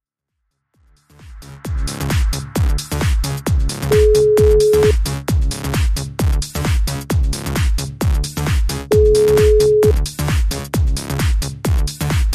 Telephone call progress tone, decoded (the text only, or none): ringback tone